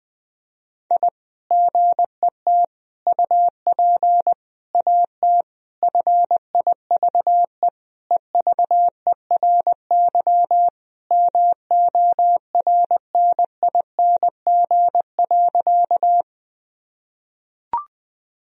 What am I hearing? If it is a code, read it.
Morse code, heard as a keyed tone, about 21 wpm, 700 Hz. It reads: I GET UP AT FIVE EVERY MORNING.
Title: Sentences from Top 1000 Words - Review Part 2 - 20wpm